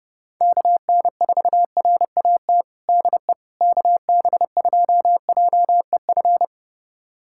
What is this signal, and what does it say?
Morse code, heard as a keyed tone, about 32 wpm, 700 Hz. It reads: KN4RAT DE KB2JEF
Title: USA Callsign DE Callsign - 6 Characters - 30wpm